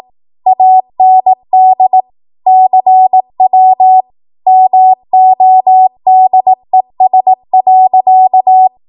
Morse code, heard as a keyed tone, about 18 wpm, 750 Hz. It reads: AND CW MODES.
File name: Morse / 18WPM